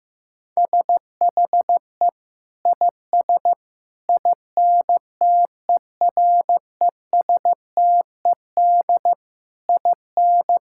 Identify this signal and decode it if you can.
Morse code, heard as a keyed tone, about 16 wpm, 700 Hz. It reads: SHE IS INTERESTED IN
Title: Sentences from Top 1000 Words - Review Part 6 - 15wpm